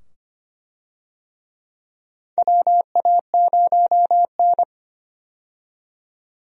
Morse code, heard as a keyed tone, about 25 wpm, 700 Hz. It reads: WA0N
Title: USA Call Signs - 4 Characters - 25wpm